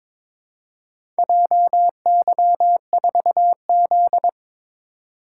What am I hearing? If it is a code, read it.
Morse code, heard as a keyed tone, about 23 wpm, 700 Hz. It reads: JY4Z
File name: practice / 22wpm / Four Letter-Number Combinations - 22wpm